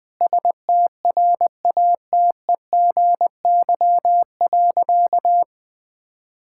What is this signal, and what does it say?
Morse code, heard as a keyed tone, about 20 wpm, 700 Hz. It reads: STRATEGY.
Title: Sentences from Top 1000 Words - Review Part 5 - 20wpm